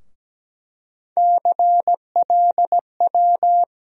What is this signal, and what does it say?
Morse code, heard as a keyed tone, about 17 wpm, 700 Hz. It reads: CLW
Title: USA Call Sign Postfixes - 3 Letters 17wpm